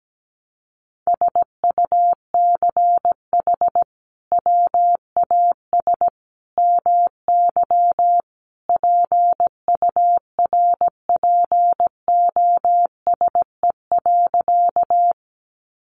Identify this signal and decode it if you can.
Morse code, heard as a keyed tone, about 17 wpm, 700 Hz. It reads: SUCH WAS MY PURPOSE.
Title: Sentences from Top 800 Words 17wpm